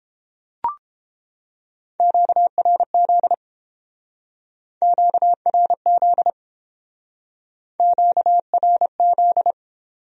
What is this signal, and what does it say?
Morse code, heard as a keyed tone, about 30 wpm, 700 Hz. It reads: QRZ QRZ QRZ